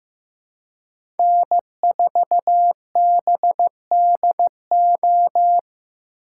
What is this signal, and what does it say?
Morse code, heard as a keyed tone, about 15 wpm, 700 Hz. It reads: N4BDO
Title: USA Call Signs - 5 Characters - 3x Character Spacing - 15wpm